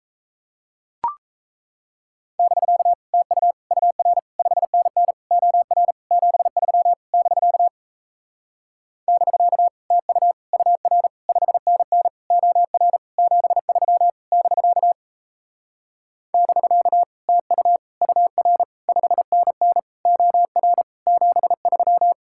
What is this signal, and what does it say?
Morse code, heard as a keyed tone, about 38 wpm, 700 Hz. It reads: <BK> TU UR 5NN OR 73 <BK> <BK> TU UR 5NN OR 73 <BK> <BK> TU UR 5NN OR 73